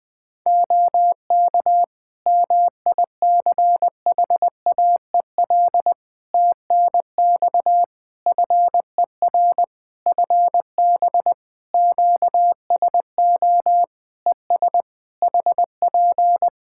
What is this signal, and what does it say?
Morse code, heard as a keyed tone, about 20 wpm, 700 Hz. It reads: OK MICHAEL TNX FER FB QSO ES HP
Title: K1USN SST QSO Practice 20wpm